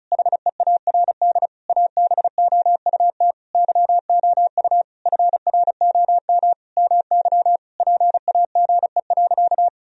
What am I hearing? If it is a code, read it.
Morse code, heard as a keyed tone, about 36 wpm, 700 Hz. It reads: HEARD ABOUT YOU FROM MY PAGE.
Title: Sentences from Top 1000 Words - Review Part 1 - 35wpm